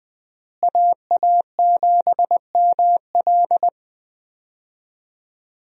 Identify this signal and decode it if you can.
Morse code, heard as a keyed tone, about 20 wpm, 700 Hz. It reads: AA7ML
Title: USA Call Signs - 5 Characters - Repeated 3x - 20wpm